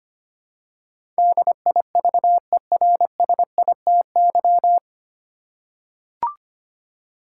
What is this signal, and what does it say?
Morse code, heard as a keyed tone, about 25 wpm, 690 Hz. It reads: DIVERSITY
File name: practice / 25wpm / 9 Letter Words - First 3 Letters Given - 25wpm